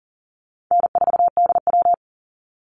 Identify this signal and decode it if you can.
Morse code, heard as a keyed tone, about 40 wpm, 700 Hz. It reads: N4DW
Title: Easy Call Signs Repeated 2x - 40wpm